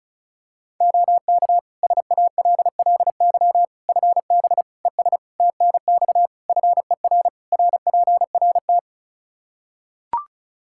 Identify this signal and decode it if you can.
Morse code, heard as a keyed tone, about 36 wpm, 700 Hz. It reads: OK SALLY FB ES TNX FER RPRT
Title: Practice QSO - Exchange 2 - Info - 35wpm